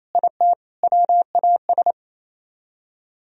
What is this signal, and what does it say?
Morse code, heard as a keyed tone, about 35 wpm, 700 Hz. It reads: IT WAS